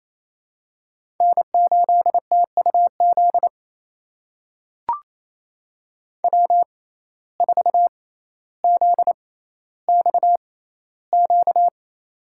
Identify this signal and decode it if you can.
Morse code, heard as a keyed tone, about 28 wpm, spaced as for 14 wpm, 700 Hz. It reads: N8TUZ W4ZXQ
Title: USA Call Signs - 5 Characters - 6x Character Spacing - 28wpm